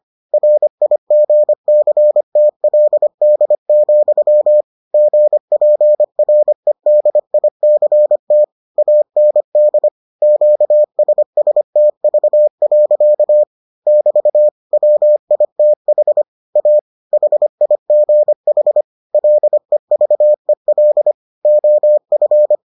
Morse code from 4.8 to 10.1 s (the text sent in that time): GPREDICT AND